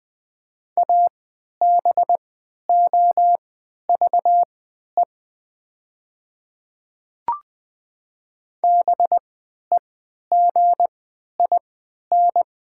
Morse code, heard as a keyed tone, about 20 wpm, spaced as for 11 wpm, 700 Hz. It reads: ABOVE BEGIN
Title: 3x Character Spacing - Top 500 Words - 20wpm